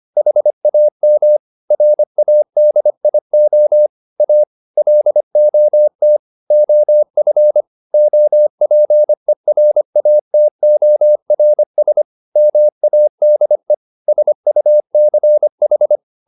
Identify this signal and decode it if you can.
Morse code, heard as a keyed tone, about 25 wpm, 600 Hz. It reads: HAM RADIO A LOT OF OPERATORS MADE SUCH